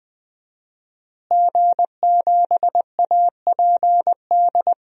Morse code, heard as a keyed tone, about 20 wpm, 700 Hz. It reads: G7APD
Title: DXCC-100 Call Signs - 5 Characters - Repeated 3x - 20wpm